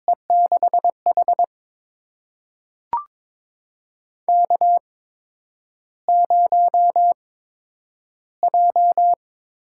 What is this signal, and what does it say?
Morse code, heard as a keyed tone, about 22 wpm, spaced as for 9 wpm, 700 Hz. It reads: E6H K0J
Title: USA Call Signs - 4 Characters - 8x Character Spacing - 22wpm